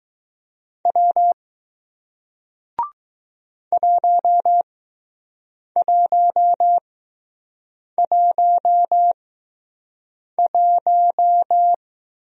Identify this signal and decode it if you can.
Morse code, heard as a keyed tone, about 20 wpm, spaced as for 6 wpm, 700 Hz. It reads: W 1111